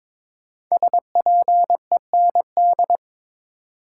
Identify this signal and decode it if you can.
Morse code, heard as a keyed tone, about 22 wpm, 700 Hz. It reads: SPEND
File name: practice / 22wpm / 5 Letter Words - First 3 Letters Given - 22wpm